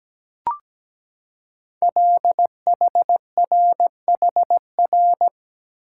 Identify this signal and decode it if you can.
Morse code, heard as a keyed tone, about 17 wpm, 700 Hz. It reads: LHRHR
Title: Code Groups - 5 Letter:Numbers - 17wpm